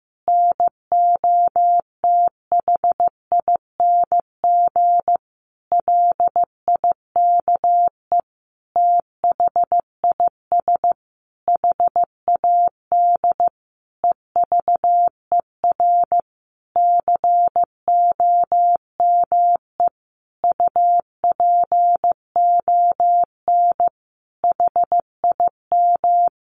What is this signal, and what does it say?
Morse code, heard as a keyed tone, about 15 wpm, 700 Hz. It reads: NOTHING LIKE THIS HAD EVER COME UPON HIM
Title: Sentences from Top 1100 Words 15wpm